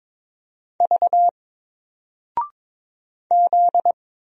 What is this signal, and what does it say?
Morse code, heard as a keyed tone, about 22 wpm, 700 Hz. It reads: V Z
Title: Single Character (Letter:Number:Punctuation:Prosign) Alt 22wpm